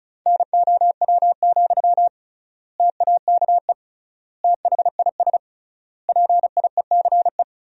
Morse code, heard as a keyed tone, about 36 wpm, 700 Hz. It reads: NOW, TAKE THIS PIECE
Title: 3x Word-Spacing - Sentences from Top 400 Words - 35wpm